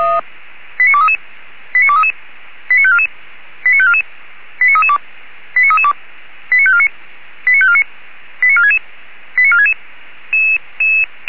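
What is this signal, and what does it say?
Five-tone selective call (ZVEI), heard as quick runs of five tones, tones of about 70 ms. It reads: 89233 89233 89544 89544 89332 89332 89549 89549 89466 89466
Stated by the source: ZVEI five-tone call 89288 received by SDR, from feuerwehr-orlamuende.org (Orlamünde fire brigade) audio archive.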